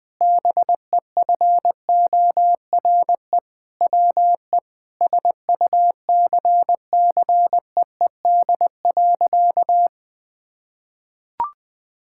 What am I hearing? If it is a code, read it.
Morse code, heard as a keyed tone, about 20 wpm, 700 Hz. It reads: BEFORE WE SUCCEED.
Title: Sentences with 6-Word Phrases - 20wpm